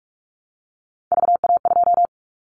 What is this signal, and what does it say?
Morse code, heard as a keyed tone, about 46 wpm, 700 Hz. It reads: VA2
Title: Callsign Prefixes for Top 25 DXCC - 45wpm